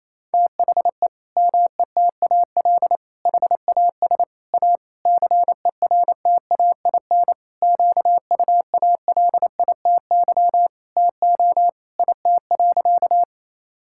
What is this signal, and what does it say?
Morse code, heard as a keyed tone, about 28 wpm, 700 Hz. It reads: THE METAL HAS A CERTAIN QUALITY TO IT.
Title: Sentences from Top 1000 Words - Review Part 6 - 28wpm